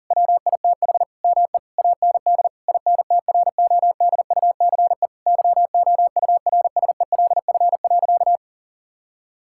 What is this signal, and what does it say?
Morse code, heard as a keyed tone, about 37 wpm, 700 Hz. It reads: WITH ME AND INTRODUCE YOURSELF.
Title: Sentences from Top 1600 Words 40wpm